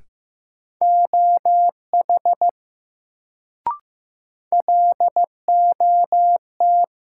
Morse code, heard as a keyed tone, about 15 wpm, 700 Hz. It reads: OH LOT
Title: Sentences from Top 500 Words - New Word Drill - 15wpm